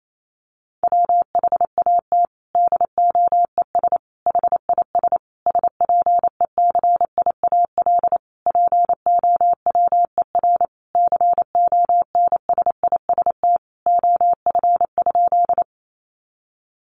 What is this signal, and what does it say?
Morse code, heard as a keyed tone, about 28 wpm, 700 Hz. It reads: WHAT DOES HIS SPECIAL POWER CONSIST OF?
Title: Sentences from Top 1400 Words 28wpm